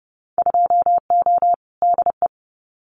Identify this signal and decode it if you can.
Morse code, heard as a keyed tone, about 30 wpm, 700 Hz. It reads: 2O DE
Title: Winter Field Day Exchanges Encore 30wpm